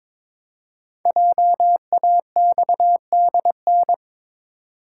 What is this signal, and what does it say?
Morse code, heard as a keyed tone, about 22 wpm, 690 Hz. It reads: JAXDN
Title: Code Groups - 5 Letters - 22wpm